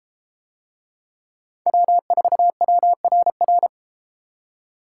Morse code, heard as a keyed tone, about 32 wpm, 700 Hz. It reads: W4WRR